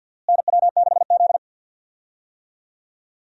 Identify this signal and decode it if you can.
Morse code, heard as a keyed tone, about 51 wpm, 700 Hz. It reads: NW6Z
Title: USA Call Signs - 4 Characters - Repeated 3x - 50wpm